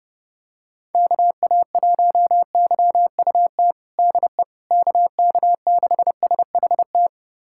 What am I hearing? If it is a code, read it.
Morse code, heard as a keyed tone, about 30 wpm, 700 Hz. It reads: KA1YUT DE KK6SHT
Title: USA Callsign DE Callsign 30wpm